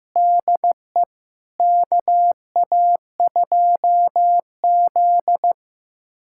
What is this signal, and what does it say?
Morse code, heard as a keyed tone, about 15 wpm, 700 Hz. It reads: DE KA2Z
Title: USA Callsign DE Callsign - 4 Characters - 15wpm